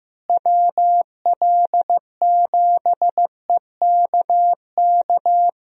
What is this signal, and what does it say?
Morse code, heard as a keyed tone, about 15 wpm, 700 Hz. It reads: WL7EKK
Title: USA Call Signs - 6 Characters - Repeated 3x - 15wpm